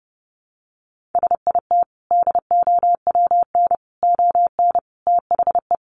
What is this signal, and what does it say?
Morse code, heard as a keyed tone, about 30 wpm, 700 Hz. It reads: SIT DOWN ON THE
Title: Sentences from Top 1400 Words 30wpm